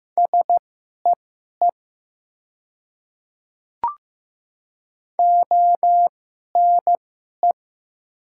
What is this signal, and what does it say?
Morse code, heard as a keyed tone, about 15 wpm, spaced as for 11 wpm, 700 Hz. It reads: SEE ONE
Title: 2x Character Spacing - Top 100 Words - 15wpm